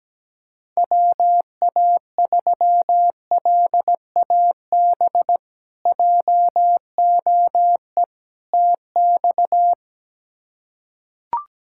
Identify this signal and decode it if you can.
Morse code, heard as a keyed tone, about 17 wpm, 700 Hz. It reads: WA3LAB JOE TX
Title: K1USN SST QSO Practice - 17wpm